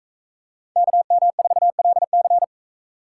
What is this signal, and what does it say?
Morse code, heard as a keyed tone, about 44 wpm, 700 Hz. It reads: KM4LC